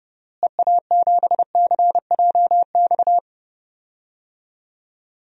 Morse code, heard as a keyed tone, about 30 wpm, 700 Hz. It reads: EA7CJX
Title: DXCC-100 Call Signs - 6 Characters - Repeated 3x - 30wpm